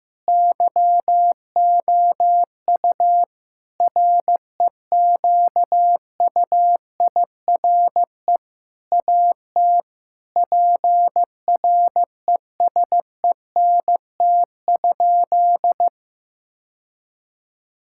Morse code, heard as a keyed tone, about 15 wpm, 700 Hz. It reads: YOU REQUIRE AT PRESENT?
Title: Sentences from Top 1000 Words - Review Part 6 - 15wpm